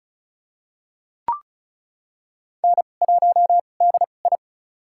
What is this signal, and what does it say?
Morse code, heard as a keyed tone, about 36 wpm, spaced as for 25 wpm, 700 Hz. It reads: N1DI
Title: USA Call Signs - 4 Characters - 2x Character Spacing - 35wpm